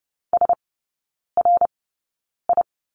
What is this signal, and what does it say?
Morse code, heard as a keyed tone, about 30 wpm, spaced as for 9 wpm, 700 Hz. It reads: SRI
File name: Lesson 18e - Letter m - Three Characters 30wpm